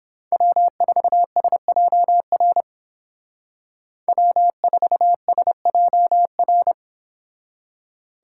Morse code, heard as a keyed tone, about 29 wpm, 700 Hz. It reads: W4SJR W4SJR